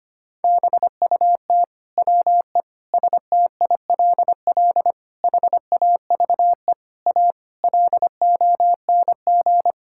Morse code, heard as a keyed tone, about 25 wpm, 700 Hz. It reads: BUT WE STILL HAVE A LONG
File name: Sets of 9 Words - 25wpm